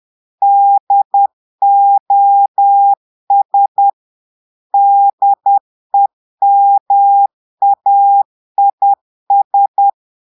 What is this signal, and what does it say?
Morse code, heard as a keyed tone, about 10 wpm, 800 Hz. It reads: DOS DEMAIS
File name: Portugais - texte en clair 1 - vitesse - 10 mots minute